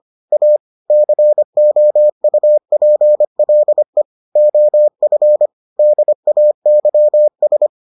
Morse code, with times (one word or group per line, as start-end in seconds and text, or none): 0.3-0.6 A
0.9-4.0 COUPLE
4.4-5.5 OF
5.8-7.7 DAYS